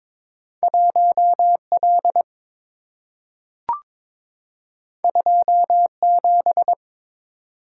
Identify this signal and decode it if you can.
Morse code, heard as a keyed tone, about 22 wpm, 700 Hz. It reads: 1L 27
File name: practice / 22wpm / Two Letter-Number Combinations - 22wpm